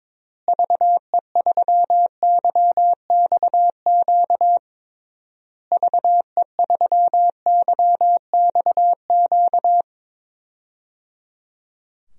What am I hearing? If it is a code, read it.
Morse code, heard as a keyed tone, about 22 wpm, 700 Hz. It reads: VE3YXQ VE3YXQ
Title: DXCC-100 Call Signs - 6 Characters - Repeated 3x - 22wpm